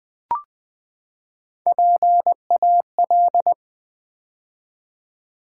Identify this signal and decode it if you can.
Morse code, heard as a keyed tone, about 20 wpm, 700 Hz. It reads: PAL
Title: Build A Word - 3 Letters At A Time - 6-Letter Words - 20wpm